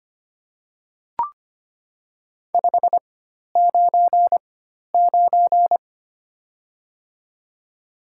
Morse code, heard as a keyed tone, about 25 wpm, spaced as for 12 wpm, 700 Hz. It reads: 599
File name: Top 90 QSO Elements - 4x Character Spacing - 25wpm